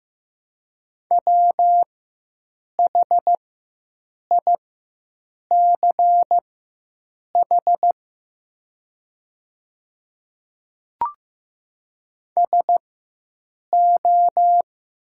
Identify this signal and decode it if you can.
Morse code, heard as a keyed tone, about 15 wpm, spaced as for 7 wpm, 700 Hz. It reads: WHICH SO
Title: 4x Character Spacing - Top 100 Words - 15wpm